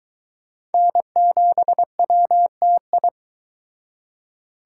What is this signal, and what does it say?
Morse code, heard as a keyed tone, about 22 wpm, 700 Hz. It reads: N7WTI